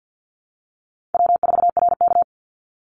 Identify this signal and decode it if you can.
Morse code, heard as a keyed tone, about 50 wpm, 700 Hz. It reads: W4RK
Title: Easy Call Signs Repeated 3x - 50wpm